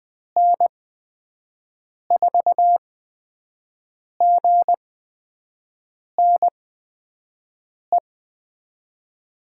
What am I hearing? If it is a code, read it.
Morse code, heard as a keyed tone, about 20 wpm, spaced as for 5 wpm, 700 Hz. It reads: N4GNE
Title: USA Call Signs - 8x Character Spacing - 20wpm